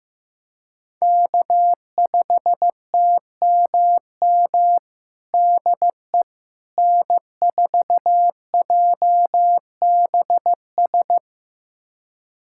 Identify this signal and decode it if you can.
Morse code, heard as a keyed tone, about 15 wpm, 700 Hz. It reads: K5TMM DE N4JBS
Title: USA Callsign DE Callsign - 5 Characters - 15wpm